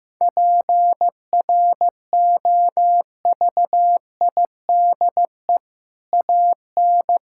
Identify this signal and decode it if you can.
Morse code, heard as a keyed tone, about 15 wpm, 700 Hz. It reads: PROVIDE AN
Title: Sentences from Top 1000 Words - Review Part 5 - 15wpm